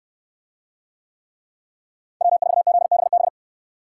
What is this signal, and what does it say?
Morse code, heard as a keyed tone, about 69 wpm, 700 Hz. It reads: W4ZBB